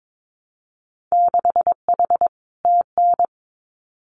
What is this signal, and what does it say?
Morse code, heard as a keyed tone, about 22 wpm, 700 Hz. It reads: 6H TN
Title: Winter Field Day Exchanges Encore 22wpm